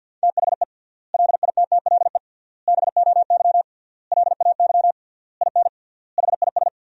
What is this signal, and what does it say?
Morse code, heard as a keyed tone, about 47 wpm, 700 Hz. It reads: THE LITTLE BOY LAY IN HIS
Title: 3x Word-Spacing - Sentences from Top 500 Words - 50wpm